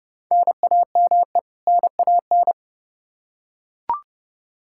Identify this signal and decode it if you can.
Morse code, heard as a keyed tone, about 30 wpm, 700 Hz. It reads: NAME NAN
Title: Practice QSO - Exchange 1 - Beginning - 30wpm